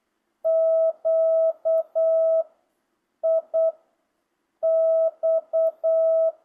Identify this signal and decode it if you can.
Morse code, heard as a keyed tone, about 8 wpm, 650 Hz. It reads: Q I X